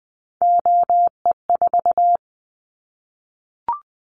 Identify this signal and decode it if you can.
Morse code, heard as a keyed tone, about 20 wpm, 700 Hz. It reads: OE4
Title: Callsign Prefixes for Top 50 DXCC - 20wpm